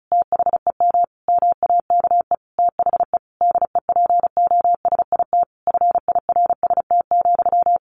Morse code, heard as a keyed tone, about 34 wpm, 700 Hz. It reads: THEM MAKE THE DEPOSIT FIRST,
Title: Sentences from Top 1300 Words 35wpm